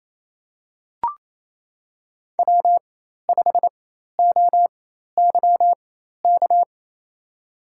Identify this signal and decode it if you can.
Morse code, heard as a keyed tone, about 27 wpm, spaced as for 13 wpm, 700 Hz. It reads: W5OYK
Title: USA Call Signs - 5 Characters - 4x Character Spacing - 28wpm